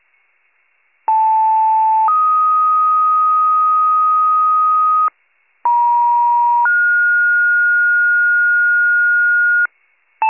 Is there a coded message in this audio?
Yes: two-tone page from 1.1 to 9.7 s, heard as a one-second tone, then a three-second tone, about 880 then 1250 Hz, about 950 then 1430 Hz, with noise.